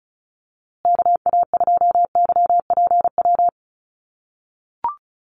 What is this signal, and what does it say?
Morse code, heard as a keyed tone, about 34 wpm, 700 Hz. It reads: KA2YPW
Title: USA Call Signs Repeated 3x 35wpm